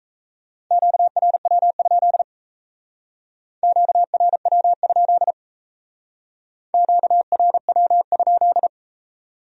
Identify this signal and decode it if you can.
Morse code, heard as a keyed tone, about 34 wpm, 700 Hz. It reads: QRW? QRW? QRW?
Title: Speed-Racing Q-Codes - 28wpm